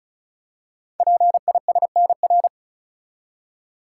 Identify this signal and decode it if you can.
Morse code, heard as a keyed tone, about 35 wpm, 690 Hz. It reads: PISNR
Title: Code Groups - 5 Letters - 35wpm